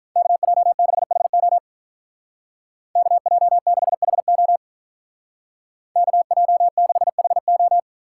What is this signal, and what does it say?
Morse code, heard as a keyed tone, about 43 wpm, 700 Hz. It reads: KJ6HO KJ6HO KJ6HO
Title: Speed Racing - USA 2x2 Call Signs 35wpm